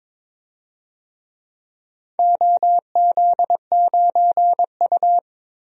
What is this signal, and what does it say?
Morse code, heard as a keyed tone, about 22 wpm, 700 Hz. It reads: OZ9U
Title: DXCC-100 Call Signs - 4 Characters - Repeated 3x - 22wpm